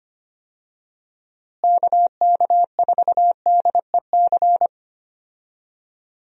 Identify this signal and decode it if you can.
Morse code, heard as a keyed tone, about 25 wpm, 700 Hz. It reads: KK4DEC
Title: USA Call Signs - 6 Characters - Repeated 2x - 25wpm